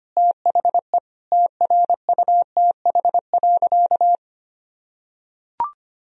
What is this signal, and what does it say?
Morse code, heard as a keyed tone, about 25 wpm, 700 Hz. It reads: THE TRUTH.
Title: Sentences from Top 1000 Words - Review Part 3 - 25wpm